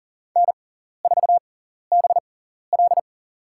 Morse code, heard as a keyed tone, about 40 wpm, spaced as for 14 wpm, 700 Hz. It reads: N4BL
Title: USA Call Signs - 4 Characters - 6x Character Spacing - 40wpm